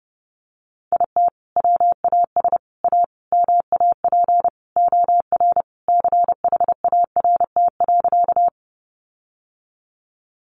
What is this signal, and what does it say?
Morse code, heard as a keyed tone, about 30 wpm, 700 Hz. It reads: IT WAS A MAP OR CHART.
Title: Sentences from Top 1300 Words 30wpm